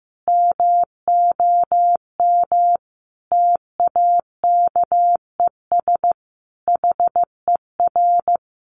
Morse code, heard as a keyed tone, about 15 wpm, 700 Hz. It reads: MOM TAKES HER
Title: Additional Easy Phrases 15wpm